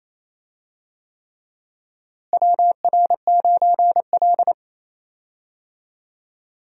Morse code, heard as a keyed tone, about 28 wpm, 700 Hz. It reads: WR9L
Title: USA Call Signs - 4 Characters - Repeated 3x - 28wpm